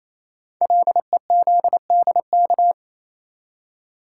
Morse code, heard as a keyed tone, about 28 wpm, 700 Hz. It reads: LEZDK
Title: Code Groups - 5 Letters - 28wpm